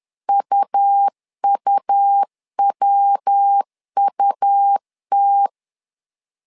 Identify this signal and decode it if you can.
Morse code, heard as a keyed tone, about 10 wpm, 800 Hz. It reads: UUWUT
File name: LEZIONI / 3 MWRO / 3 MW 10wpm 1959